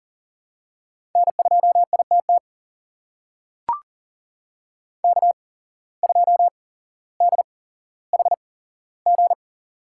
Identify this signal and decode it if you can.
Morse code, heard as a keyed tone, about 40 wpm, spaced as for 17 wpm, 700 Hz. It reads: N2ITT K2DHG
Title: USA Call Signs - 8x Character Spacing - 40wpm